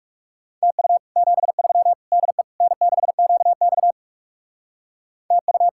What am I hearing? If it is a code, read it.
Morse code, heard as a keyed tone, about 36 wpm, 700 Hz. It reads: TU 73 DE N6QX TU
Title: Speed-Racing Common QSO Phrases Encore - 30wpm